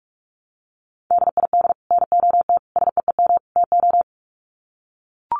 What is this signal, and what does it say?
Morse code, heard as a keyed tone, about 44 wpm, 700 Hz. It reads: DID NOT SEEM TO
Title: Sets of 4 Words 45wpm